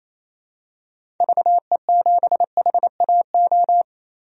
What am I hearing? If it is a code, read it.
Morse code, heard as a keyed tone, about 28 wpm, 700 Hz. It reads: VE7HAO
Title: DXCC-100 Call Signs - 6 Characters - 28wpm